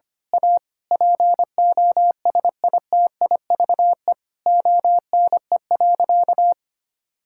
Morse code, heard as a keyed tone, about 25 wpm, 700 Hz. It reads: A POSITIVE ONE.
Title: Sentences from Top 1000 Words - Review Part 6 - 25wpm